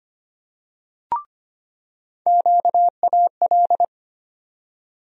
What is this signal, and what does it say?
Morse code, heard as a keyed tone, about 25 wpm, 700 Hz. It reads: QAL
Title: USA Call Sign Postfixes - 3 Letters 25wpm